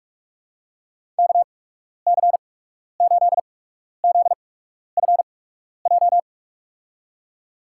Morse code, heard as a keyed tone, about 48 wpm, spaced as for 12 wpm, 700 Hz. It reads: KC8ZFJ
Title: USA Call Signs - 6 Characters - 8x Character Spacing - 45wpm